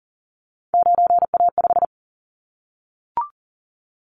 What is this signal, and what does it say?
Morse code, heard as a keyed tone, about 42 wpm, 690 Hz. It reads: 9A5
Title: Callsign Prefixes for Top 100 DXCC - 40wpm